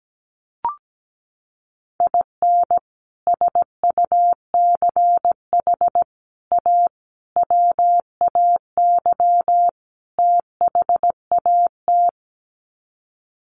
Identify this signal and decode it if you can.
Morse code, heard as a keyed tone, about 17 wpm, 700 Hz. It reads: IN SUCH A WAY THAT
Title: Sets of 5 Words 17wpm